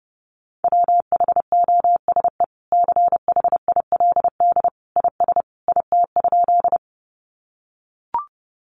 Morse code, heard as a keyed tone, about 30 wpm, 700 Hz. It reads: WHOSE CHILD IS IT?
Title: Sentences from Top 1200 Words 30wpm